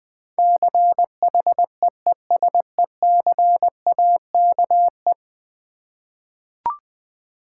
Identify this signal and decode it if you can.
Morse code, heard as a keyed tone, about 21 wpm, 700 Hz. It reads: CHEESECAKE
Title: Difficult Words with Hints - 20wpm